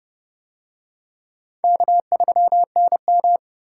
Morse code, heard as a keyed tone, about 28 wpm, 690 Hz. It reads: K3NM